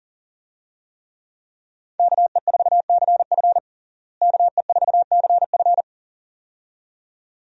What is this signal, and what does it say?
Morse code, heard as a keyed tone, about 40 wpm, 700 Hz. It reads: KE4CF KE4CF
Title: USA Call Signs - 5 Characters - Repeated 2x - 40wpm